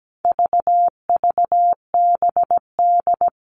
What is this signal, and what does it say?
Morse code, heard as a keyed tone, about 17 wpm, 700 Hz. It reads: VVBD
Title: All Cut Numbers - 4-Digit 17wpm